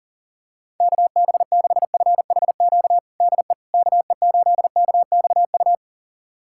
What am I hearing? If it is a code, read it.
Morse code, heard as a keyed tone, about 40 wpm, 700 Hz. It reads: KB6FHQ DE KE8KXU
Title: USA Callsign DE Callsign 40wpm